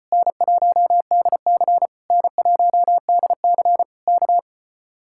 Morse code, heard as a keyed tone, about 33 wpm, 700 Hz. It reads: N1DC N1DC K